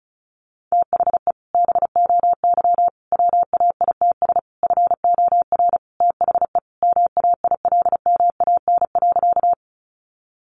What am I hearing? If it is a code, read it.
Morse code, heard as a keyed tone, about 34 wpm, 700 Hz. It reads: THE BOY WAITS FOR THE MAILMAN.
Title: Additional Easy Phrases 35wpm